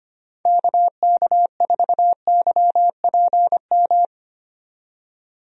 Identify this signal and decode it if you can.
Morse code, heard as a keyed tone, about 25 wpm, 700 Hz. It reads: KK4YPM
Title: USA Call Signs - 6 Characters - 25wpm